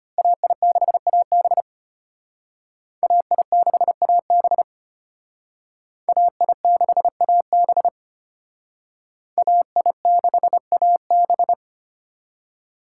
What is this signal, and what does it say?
Morse code, heard as a keyed tone, about 32 wpm, 700 Hz. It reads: AI6AB AI6AB AI6AB AI6AB